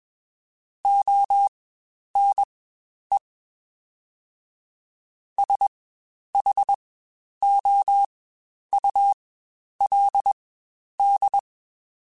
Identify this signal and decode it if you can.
Morse code, heard as a keyed tone, about 21 wpm, spaced as for 10 wpm, 800 Hz. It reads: ONE SHOULD
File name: Quotes Spaced / Quotes Spaced-mp3s / Advice sp @20